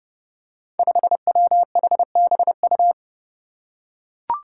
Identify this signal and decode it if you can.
Morse code, heard as a keyed tone, about 30 wpm, 700 Hz. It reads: 5WHBU
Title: Five Challenging Letter-Number Combinations 30wpm